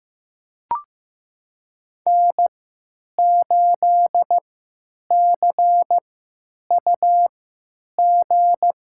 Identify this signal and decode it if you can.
Morse code, heard as a keyed tone, about 15 wpm, spaced as for 9 wpm, 700 Hz. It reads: N8CUG
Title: USA Call Signs - 5 Characters - 3x Character Spacing - 15wpm